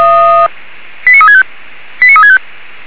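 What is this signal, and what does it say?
Five-tone selective call (ZVEI), heard as quick runs of five tones, tones of about 70 ms. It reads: 89365 89365
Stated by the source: ZVEI five-tone call received by SDR.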